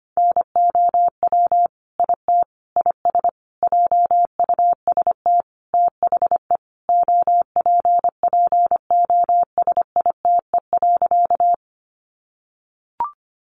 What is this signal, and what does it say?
Morse code, heard as a keyed tone, about 25 wpm, 700 Hz. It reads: NOW IT IS JUST THE OPPOSITE.
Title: Sentences from Top 1300 Words 25wpm